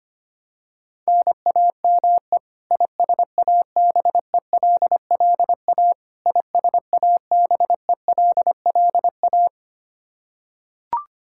Morse code, heard as a keyed tone, about 25 wpm, 700 Hz. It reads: NAME ISABELLA ISABELLA
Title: Practice QSO - Exchange 1 - Beginning - 25wpm